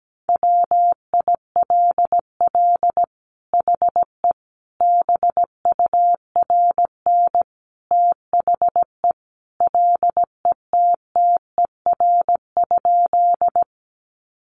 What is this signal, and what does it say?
Morse code, heard as a keyed tone, about 17 wpm, 700 Hz. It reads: WILL HE BURN THE LETTER?
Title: Sentences from Top 1300 Words 17wpm